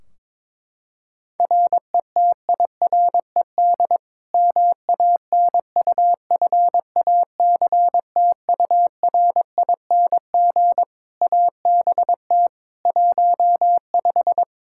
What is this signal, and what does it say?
Morse code, heard as a keyed tone, about 22 wpm, 700 Hz. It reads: RETIRED MANUFACTURING ABT 15
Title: Practice QSO - Exchange 3 - More Info - 22wpm